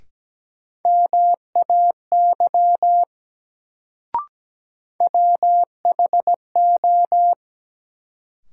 Morse code, heard as a keyed tone, about 17 wpm, 700 Hz. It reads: MAY WHO
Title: Sentences from Top 100 Words - New Word Drill - 17wpm